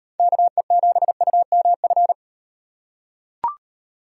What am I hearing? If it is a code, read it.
Morse code, heard as a keyed tone, about 38 wpm, 700 Hz. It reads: KE7UMF